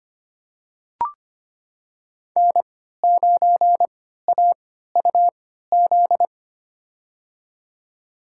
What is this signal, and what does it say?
Morse code, heard as a keyed tone, about 25 wpm, spaced as for 14 wpm, 700 Hz. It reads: N9AUZ
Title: USA Call Signs - 5 Characters - 3x Character Spacing - 25wpm